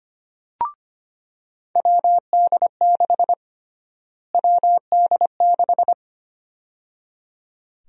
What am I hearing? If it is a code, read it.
Morse code, heard as a keyed tone, about 25 wpm, 700 Hz. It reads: WD6 WD6
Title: USA Call Sign Prefixes - Repeated 2x - 25wpm